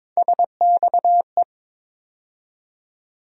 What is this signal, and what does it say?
Morse code, heard as a keyed tone, about 22 wpm, 700 Hz. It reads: SXE
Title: USA Call Sign Postfixes - 3 Letters 22wpm